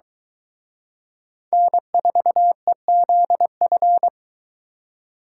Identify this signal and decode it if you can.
Morse code, heard as a keyed tone, about 23 wpm, 700 Hz. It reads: N4EZF